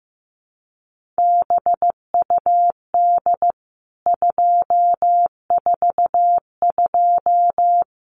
Morse code, heard as a keyed tone, about 15 wpm, 700 Hz. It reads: BUD 242
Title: CWT Exchange 15wpm